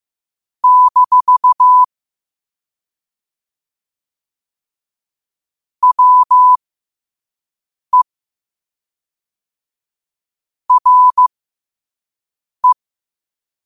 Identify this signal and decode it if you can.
Morse code, heard as a keyed tone, about 15 wpm, spaced as for 5 wpm, 1000 Hz. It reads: - WE RE
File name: qotdfiles / 5wpm